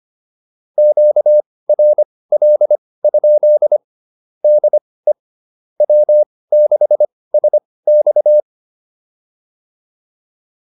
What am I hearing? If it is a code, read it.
Morse code, heard as a keyed tone, about 25 wpm, spaced as for 18 wpm, 600 Hz. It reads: QRL? DE W6SX
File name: QSO102 18